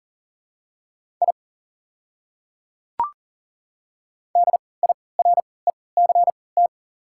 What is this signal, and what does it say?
Morse code, heard as a keyed tone, about 40 wpm, spaced as for 23 wpm, 700 Hz. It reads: I DIRECT
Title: 3x Character Spacing - Top 500 Words - 40wpm